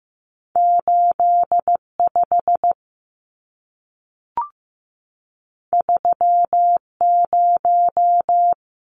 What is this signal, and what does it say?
Morse code, heard as a keyed tone, about 15 wpm, 700 Hz. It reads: 85 30
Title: Numbers - 2-Digit 15wpm